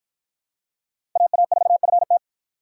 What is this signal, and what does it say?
Morse code, heard as a keyed tone, about 46 wpm, 700 Hz. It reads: AA4FT